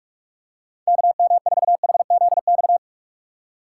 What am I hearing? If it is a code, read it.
Morse code, heard as a keyed tone, about 46 wpm, 700 Hz. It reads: KM4HZX
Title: USA Call Signs - 6 Characters - 3x Character Spacing - 45wpm